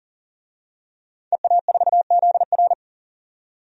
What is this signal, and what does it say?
Morse code, heard as a keyed tone, about 40 wpm, 700 Hz. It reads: EA4ZR
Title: DXCC-100 Call Signs - 5 Characters - 40wpm